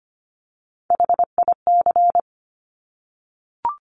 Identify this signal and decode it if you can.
Morse code, heard as a keyed tone, about 25 wpm, 700 Hz. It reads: HIC
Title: Top Three Letter Combinations in Words 25wpm